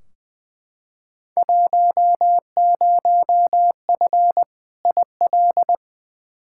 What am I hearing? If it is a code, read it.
Morse code, heard as a keyed tone, about 20 wpm, 700 Hz. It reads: 10F IL
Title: Field Day Exchanges Encore - 20wpm